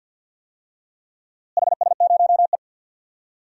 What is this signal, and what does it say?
Morse code, heard as a keyed tone, about 47 wpm, 700 Hz. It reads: HS0E
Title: DXCC-100 Call Signs - 4 Characters - Repeated 3x - 50wpm